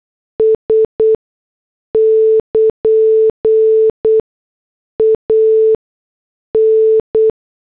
Morse code, heard as a keyed tone, about 8 wpm, spaced as for 6 wpm, 440 Hz. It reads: S <KN> AN
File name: morse / headlines